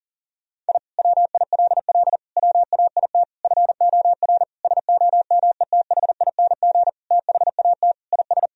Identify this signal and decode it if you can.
Morse code, heard as a keyed tone, about 40 wpm, 700 Hz. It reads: I WILL WAIT FOR SOMETHING THAT IS
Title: Sentences from Top 1000 Words - Review Part 1 - 40wpm